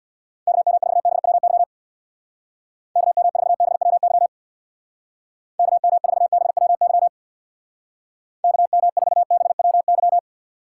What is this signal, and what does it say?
Morse code, heard as a keyed tone, about 64 wpm, 700 Hz. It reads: KM4BWY KM4BWY KM4BWY KM4BWY